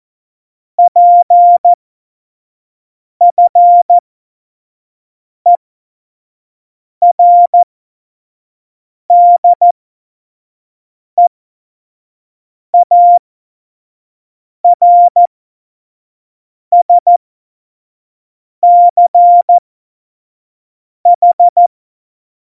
Morse code, heard as a keyed tone, about 14 wpm, spaced as for 5 wpm, 700 Hz. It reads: PFERDEARSCH
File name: Morsekurs / Witze / Witz 21 Juli 2020, 25 F70